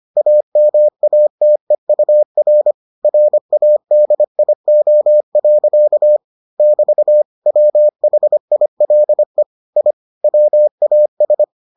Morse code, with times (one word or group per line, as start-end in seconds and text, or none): 0.2-2.7 AMATEUR
3.0-6.2 RADIO.
6.6-7.2 <BT>
7.5-9.4 WHILE
9.8-9.9 I
10.2-11.4 WAS